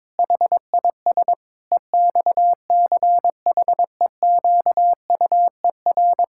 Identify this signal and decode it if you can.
Morse code, heard as a keyed tone, about 21 wpm, 700 Hz. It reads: HIS EXCHEQUER
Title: Pangram Challenge v2 - 22wpm